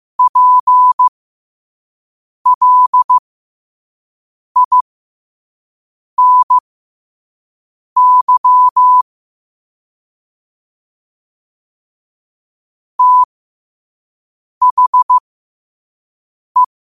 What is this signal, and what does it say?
Morse code, heard as a keyed tone, about 15 wpm, spaced as for 5 wpm, 1000 Hz. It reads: PLINY THE
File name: qotdfiles / 5wpm